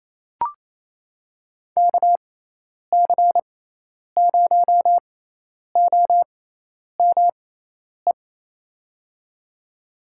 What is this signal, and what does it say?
Morse code, heard as a keyed tone, about 28 wpm, spaced as for 10 wpm, 700 Hz. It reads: KC0OME
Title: USA Call Signs - 6 Characters - 6x Character Spacing - 28wpm